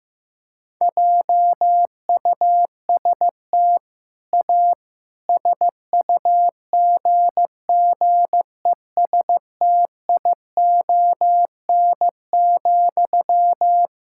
Morse code, heard as a keyed tone, about 15 wpm, 700 Hz. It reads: JUST A SUGGESTION,
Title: Sentences from Top 1600 Words 15wpm